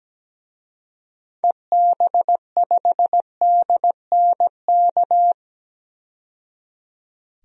Morse code, heard as a keyed tone, about 17 wpm, 700 Hz. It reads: EB5DNK
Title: DXCC-100 Call Signs - 6 Characters - 17wpm